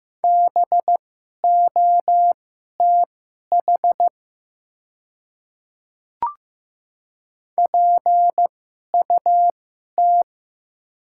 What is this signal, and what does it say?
Morse code, heard as a keyed tone, about 15 wpm, spaced as for 11 wpm, 700 Hz. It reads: BOTH PUT
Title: 2x Character Spacing - Top 300 Words - 15wpm